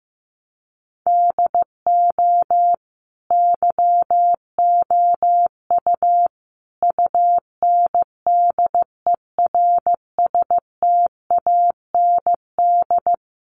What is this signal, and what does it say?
Morse code, heard as a keyed tone, about 15 wpm, 700 Hz. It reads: DO YOU UNDERSTAND
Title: Sentences from Top 600 Words 15wpm